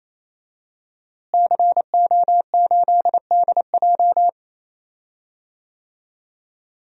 Morse code, heard as a keyed tone, about 28 wpm, 700 Hz. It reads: CO8DJ
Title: DXCC-100 Call Signs - 5 Characters - Repeated 3x - 28wpm